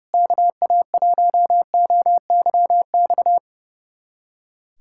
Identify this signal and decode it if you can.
Morse code, heard as a keyed tone, about 30 wpm, 700 Hz. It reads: KA1OYX
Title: USA Callsign DE Callsign - 6 Characters - 30wpm